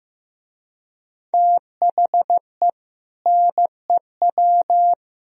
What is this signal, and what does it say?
Morse code, heard as a keyed tone, about 15 wpm, 700 Hz. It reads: THE NEW
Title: Speed-Racing - Sets of 3 Words 15wpm